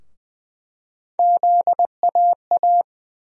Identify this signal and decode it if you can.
Morse code, heard as a keyed tone, about 20 wpm, 700 Hz. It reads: ZAA
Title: USA Call Sign Postfixes - 3 Letters 20wpm